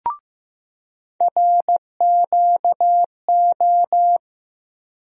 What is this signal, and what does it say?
Morse code, heard as a keyed tone, about 15 wpm, 700 Hz. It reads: RQO
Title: USA Call Sign Postfixes - 3 Letters 15wpm